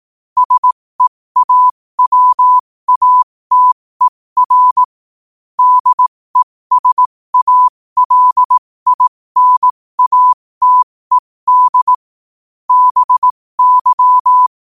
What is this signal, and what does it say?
Morse code, heard as a keyed tone, about 18 wpm, spaced as for 15 wpm, 1000 Hz. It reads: SEAWATER DESALINATED BY
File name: qotdfiles / 15wpm